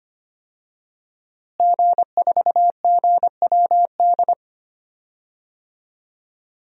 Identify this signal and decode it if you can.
Morse code, heard as a keyed tone, about 25 wpm, 700 Hz. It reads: G4GWD
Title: DXCC-100 Call Signs - 5 Characters - Repeated 2x - 25wpm